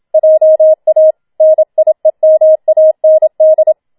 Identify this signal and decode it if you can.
Morse code, heard as a keyed tone, about 27 wpm, 630 Hz. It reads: JA NIEMAND
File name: Schneewittchen Teil2 27wpm